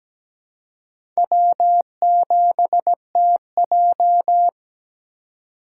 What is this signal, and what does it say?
Morse code, heard as a keyed tone, about 17 wpm, 700 Hz. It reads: W7TJ
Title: USA Call Signs - 4 Characters - 2x Character Spacing - 17wpm